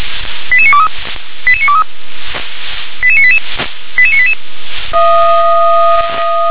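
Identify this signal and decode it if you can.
Five-tone selective call (ZVEI), heard as quick runs of five tones, tones of about 70 ms. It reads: 88923 88923 88988 88988
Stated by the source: Five-tone ZVEI selective call received on SDR.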